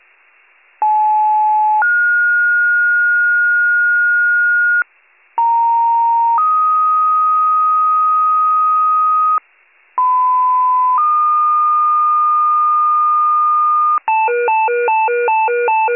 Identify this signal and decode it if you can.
Two-tone page, heard as a one-second tone, then a three-second tone, about 860 then 1430 Hz, about 930 then 1250 Hz, about 1010 then 1220 Hz, with noise. It is heard from 0.8 to 14.0 s.